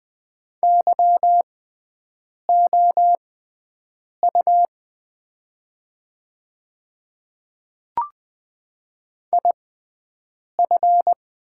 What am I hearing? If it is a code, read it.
Morse code, heard as a keyed tone, about 20 wpm, spaced as for 7 wpm, 700 Hz. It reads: YOU IF
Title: 6x Character Spacing - Top 100 Words - 20wpm